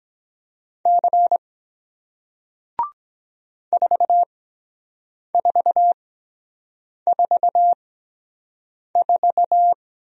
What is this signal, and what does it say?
Morse code, heard as a keyed tone, about 23 wpm, 700 Hz. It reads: C 4 4 4 4